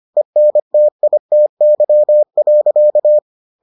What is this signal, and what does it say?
Morse code, heard as a keyed tone, about 25 wpm, 600 Hz. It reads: ENTITY.